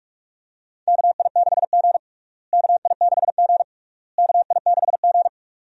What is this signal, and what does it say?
Morse code, heard as a keyed tone, about 47 wpm, 700 Hz. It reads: KI6G KI6G KI6G
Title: USA Call Signs - 4 Characters - Repeated 3x - 45wpm